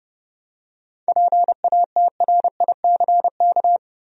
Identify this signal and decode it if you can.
Morse code, heard as a keyed tone, about 32 wpm, 700 Hz. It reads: PATRICK
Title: USA Common Names - 30wpm